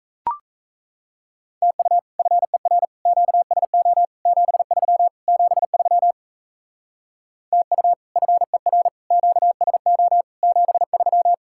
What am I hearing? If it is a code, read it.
Morse code, heard as a keyed tone, about 34 wpm, 700 Hz. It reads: TU FER QSO 73 73 TU FER QSO 73
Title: Speed-Racing Common QSO Phrases - 28wpm